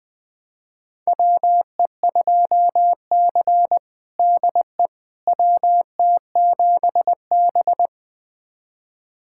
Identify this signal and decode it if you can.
Morse code, heard as a keyed tone, about 20 wpm, 700 Hz. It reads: WE2C DE WT7B
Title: USA Callsign DE Callsign - 4 Characters - 20wpm